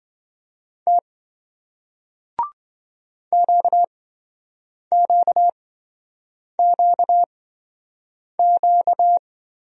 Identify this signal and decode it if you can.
Morse code, heard as a keyed tone, about 28 wpm, spaced as for 7 wpm, 700 Hz. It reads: T QQQQ